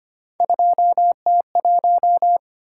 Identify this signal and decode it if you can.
Morse code, heard as a keyed tone, about 25 wpm, 700 Hz. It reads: 2T1
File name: Three Letter-Number Combinations - 25wpm